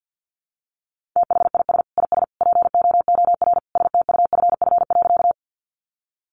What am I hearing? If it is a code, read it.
Morse code, heard as a keyed tone, about 50 wpm, 700 Hz. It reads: THIS IS POOR STUFF.
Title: Sentences from Top 1400 Words 50wpm